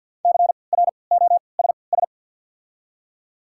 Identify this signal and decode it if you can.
Morse code, heard as a keyed tone, about 47 wpm, spaced as for 28 wpm, 700 Hz. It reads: CROSS
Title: 3x Character Spacing - Top 500 Words - 50wpm